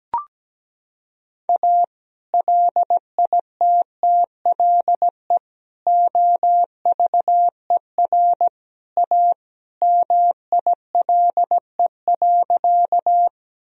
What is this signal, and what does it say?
Morse code, heard as a keyed tone, about 17 wpm, 700 Hz. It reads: A LITTLE OVER A MILE.
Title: Sentences from Top 1000 Words - Review Part 1 - 17wpm